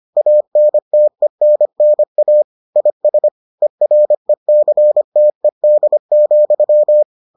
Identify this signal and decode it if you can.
Morse code, heard as a keyed tone, about 25 wpm, 600 Hz. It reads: ANTENNA IS ERECTED,